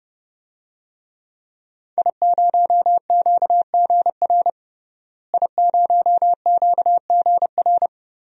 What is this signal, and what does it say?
Morse code, heard as a keyed tone, about 31 wpm, 700 Hz. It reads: I0QGR I0QGR
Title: DXCC-100 Call Signs - 5 Characters - Repeated 2x - 30wpm